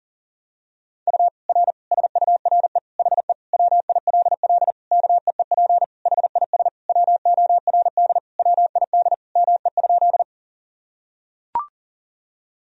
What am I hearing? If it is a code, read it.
Morse code, heard as a keyed tone, about 40 wpm, 700 Hz. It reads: U R SURE HE WILL KEEP HIS WORD WID ME?
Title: Sentences from Top 300 Words and Abbreviations 40wpm